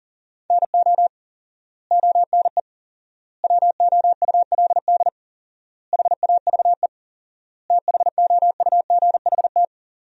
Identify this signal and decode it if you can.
Morse code, heard as a keyed tone, about 40 wpm, 700 Hz. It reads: NO ONE WOULD HAVE THOUGHT
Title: 4x Word-Spacing - Sentences from Top 300 Words - 40wpm